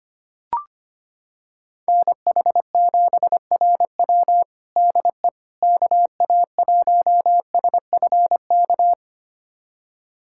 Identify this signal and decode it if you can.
Morse code, heard as a keyed tone, about 25 wpm, 700 Hz. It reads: NH7RW DE KA1SFK
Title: USA Callsign DE Callsign 25wpm